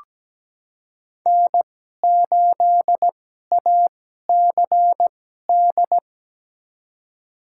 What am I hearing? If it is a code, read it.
Morse code, heard as a keyed tone, about 17 wpm, spaced as for 12 wpm, 700 Hz. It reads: N8ACD
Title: USA Call Signs - 5 Characters - 2x Character Spacing - 17wpm